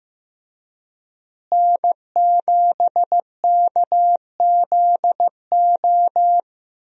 Morse code, heard as a keyed tone, about 15 wpm, 700 Hz. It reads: N7KZO
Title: USA Call Signs - 5 Characters - Repeated 3x - 15wpm